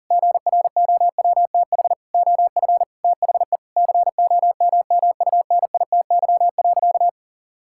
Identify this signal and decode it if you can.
Morse code, heard as a keyed tone, about 40 wpm, 700 Hz. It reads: GROWTH OF THE COMMUNITY.
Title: Sentences from Top 1000 Words - Review Part 5 - 40wpm